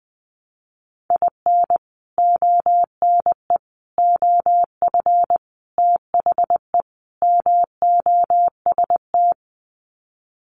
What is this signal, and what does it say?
Morse code, heard as a keyed tone, about 20 wpm, 700 Hz. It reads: IN ONE OF THE MOST
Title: Sets of 5 Words Encore 20wpm